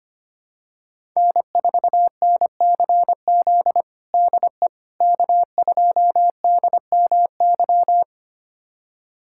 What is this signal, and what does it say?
Morse code, heard as a keyed tone, about 25 wpm, 700 Hz. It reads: N4NCZ DE K2DMY
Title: USA Callsign DE Callsign - 5 Characters - 25wpm